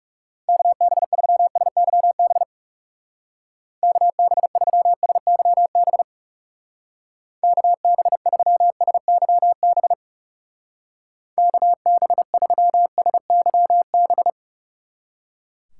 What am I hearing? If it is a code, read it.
Morse code, heard as a keyed tone, about 36 wpm, 700 Hz. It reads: KB3SYB KB3SYB KB3SYB KB3SYB
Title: Speed Racing - USA Call Signs 30wpm